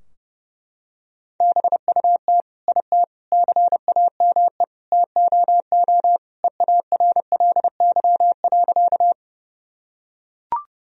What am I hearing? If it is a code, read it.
Morse code, heard as a keyed tone, about 30 wpm, 700 Hz. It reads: BUT IT CAME TOO EARLY.
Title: Sentences from Top 1000 Words - Review Part 2 - 30wpm